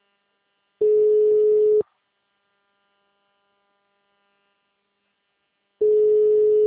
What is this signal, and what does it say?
Ringback tone (Europe, 425 Hz), with speech.
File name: out- -2030- - - .8